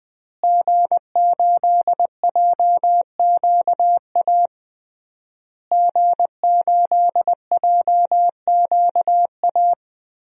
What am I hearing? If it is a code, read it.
Morse code, heard as a keyed tone, about 19 wpm, 700 Hz. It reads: G8JQA G8JQA
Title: DXCC-100 Call Signs - 5 Characters - Repeated 3x - 20wpm